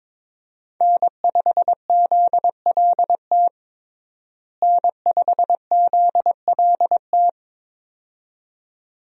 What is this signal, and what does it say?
Morse code, heard as a keyed tone, about 22 wpm, 700 Hz. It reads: N5ZLT N5ZLT
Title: USA Call Signs - 5 Characters - Repeated 2x - 22wpm